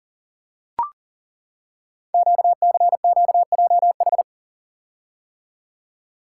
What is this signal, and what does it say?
Morse code, heard as a keyed tone, about 39 wpm, 690 Hz. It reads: QCQJH
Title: Code Groups - 5 Letters - 40wpm